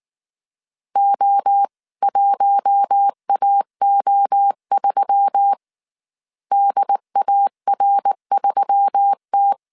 Morse code, heard as a keyed tone, about 19 wpm, 800 Hz. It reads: O1AO3 DAR3T